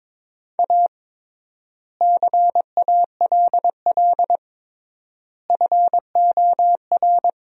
Morse code, heard as a keyed tone, about 22 wpm, 700 Hz. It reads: A CALL FOR
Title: 3x Word-Spacing - Sentences from Top 300 Words - 22wpm